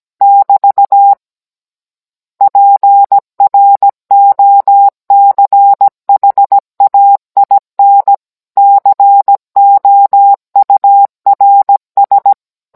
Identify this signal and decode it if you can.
Morse code, heard as a keyed tone, about 17 wpm, 800 Hz. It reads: <BT> PROCHAIN COURS